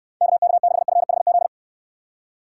Morse code, heard as a keyed tone, about 69 wpm, 700 Hz. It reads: KK6FHZ